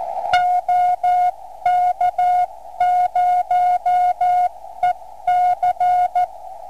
Morse code, heard as a keyed tone, about 14 wpm, 720 Hz. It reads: OK0EC